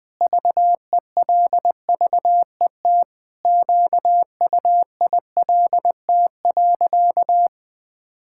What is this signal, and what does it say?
Morse code, heard as a keyed tone, about 20 wpm, 700 Hz. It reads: VELVET QUILT.
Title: Pangram Challenge v2 - 20wpm